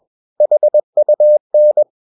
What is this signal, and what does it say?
Morse code, heard as a keyed tone, about 21 wpm, 600 Hz. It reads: HUN
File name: Ferie QRS 21wpm